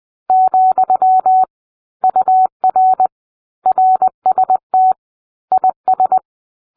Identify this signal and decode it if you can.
Morse code, heard as a keyed tone, about 20 wpm, 750 Hz. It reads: , UR RST IS